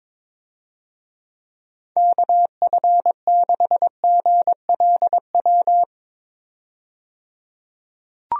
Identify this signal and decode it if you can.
Morse code, heard as a keyed tone, about 22 wpm, 700 Hz. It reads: KF6GLW
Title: USA Call Signs - 6 Characters - 22wpm